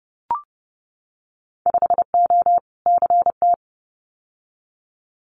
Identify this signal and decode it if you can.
Morse code, heard as a keyed tone, about 30 wpm, 700 Hz. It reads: 5O CT
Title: Winter Field Day Exchanges Encore 30wpm